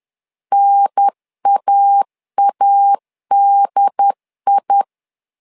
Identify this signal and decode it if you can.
Morse code, heard as a keyed tone, about 10 wpm, 800 Hz. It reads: NAADI
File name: LEZIONI / 2 UHDN / 2 UHDN 10wpm 1904